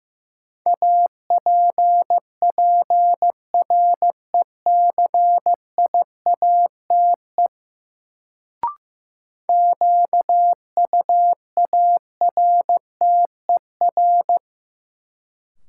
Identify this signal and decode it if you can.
Morse code, heard as a keyed tone, about 15 wpm, 700 Hz. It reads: APPRECIATE QUARTER
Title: Sentences from Top 1300 Words - New Word Drill Alt - 15wpm